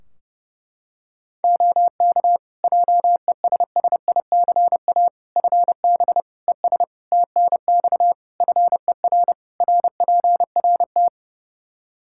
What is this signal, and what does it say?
Morse code, heard as a keyed tone, about 32 wpm, 700 Hz. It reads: OK JESSICA FB ES TNX FER RPRT
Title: Practice QSO - Exchange 2 - Info - 30wpm